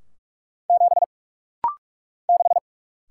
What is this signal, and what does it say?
Morse code, heard as a keyed tone, about 46 wpm, 700 Hz. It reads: 7 6
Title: Single Letter-Number - Rapid-Fire 45wpm